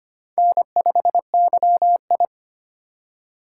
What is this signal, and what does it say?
Morse code, heard as a keyed tone, about 25 wpm, 700 Hz. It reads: N5YI
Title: USA Call Signs - 4 Characters - 3x Character Spacing - 25wpm